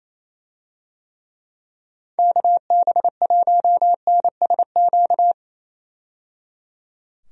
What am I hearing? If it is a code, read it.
Morse code, heard as a keyed tone, about 28 wpm, 700 Hz. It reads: KB1NSQ
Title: USA Call Signs - 6 Characters - 28wpm